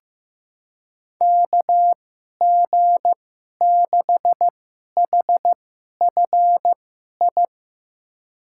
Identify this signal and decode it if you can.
Morse code, heard as a keyed tone, about 15 wpm, spaced as for 11 wpm, 700 Hz. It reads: KG6HFI
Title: USA Call Signs - 6 Characters - 2x Character Spacing - 15wpm